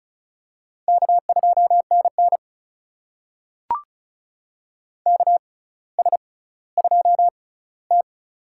Morse code, heard as a keyed tone, about 36 wpm, spaced as for 18 wpm, 700 Hz. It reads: K2NN KS2T
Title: USA Call Signs - 4 Characters - 6x Character Spacing - 35wpm